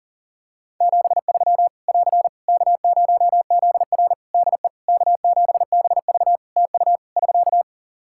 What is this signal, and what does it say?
Morse code, heard as a keyed tone, about 40 wpm, 700 Hz. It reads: 73 <AR> K0ZR DE K7BV TU <SK>
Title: K1USN SST QSO Practice 40wpm